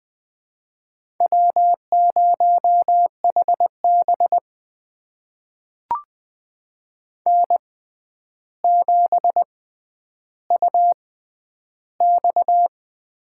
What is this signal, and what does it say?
Morse code, heard as a keyed tone, about 20 wpm, spaced as for 10 wpm, 700 Hz. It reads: W0HB N7UX
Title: USA Call Signs - 4 Characters - 6x Character Spacing - 20wpm